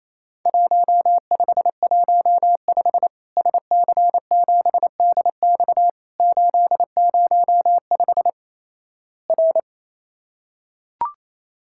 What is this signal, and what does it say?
Morse code, heard as a keyed tone, about 27 wpm, 700 Hz. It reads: 1515 SC7DX 805 R
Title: Worked All Europe DX Contest - QTC Exchange - 28wpm